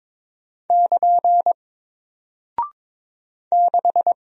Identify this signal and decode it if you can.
Morse code, heard as a keyed tone, about 22 wpm, 700 Hz. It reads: <KN> 6
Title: Single Character (Letter:Number:Punctuation:Prosign) Alt 22wpm